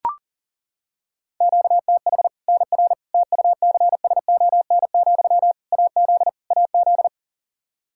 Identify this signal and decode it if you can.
Morse code, heard as a keyed tone, about 40 wpm, 700 Hz. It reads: QTH NR TUCSON, AZ AZ
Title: Practice QSO - Exchange 1 - Beginning - 40wpm